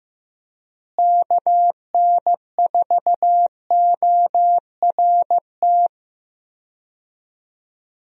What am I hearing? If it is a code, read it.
Morse code, heard as a keyed tone, about 16 wpm, 700 Hz. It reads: KN4ORT
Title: USA Call Signs - 6 Characters - Repeated 2x - 15wpm